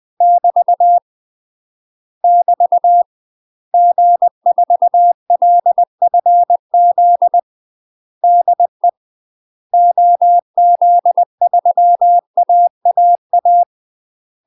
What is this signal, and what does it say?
Morse code, heard as a keyed tone, about 20 wpm, 700 Hz. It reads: <BT> <BT> G4LFZ DE OZ3AAA
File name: UK MASTER QSOs TRAINING / 20WPM / UK MASTER QSOs 0066